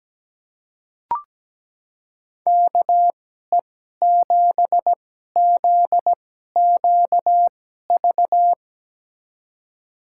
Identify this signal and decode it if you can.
Morse code, heard as a keyed tone, about 17 wpm, spaced as for 12 wpm, 700 Hz. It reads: KE7ZQV
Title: USA Call Signs - 6 Characters - 2x Character Spacing - 17wpm